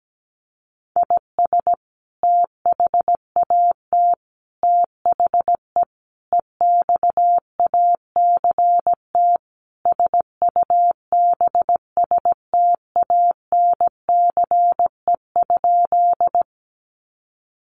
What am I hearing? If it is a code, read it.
Morse code, heard as a keyed tone, about 17 wpm, 700 Hz. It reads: IS THAT THE EXACT SUBSTANCE?
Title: Sentences from Top 1500 Words 17wpm